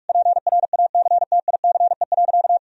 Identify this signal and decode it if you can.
Morse code, heard as a keyed tone, about 43 wpm, 700 Hz. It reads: PRACTICE.
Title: Sentences from Top 1000 Words - Review Part 4 - 45wpm